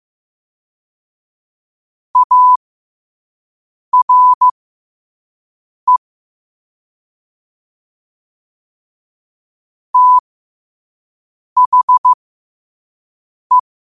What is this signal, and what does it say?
Morse code, heard as a keyed tone, about 15 wpm, spaced as for 5 wpm, 1000 Hz. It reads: ARE THE
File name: qotdfiles / 5wpm